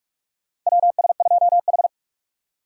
Morse code, heard as a keyed tone, about 46 wpm, 700 Hz. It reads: WS2H
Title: USA Call Signs - 6x Character Spacing - 45wpm